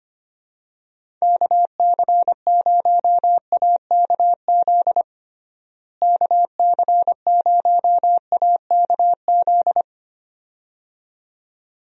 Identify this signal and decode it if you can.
Morse code, heard as a keyed tone, about 25 wpm, 700 Hz. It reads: KC0AKZ KC0AKZ
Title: USA Call Signs - 6 Characters - Repeated 2x - 25wpm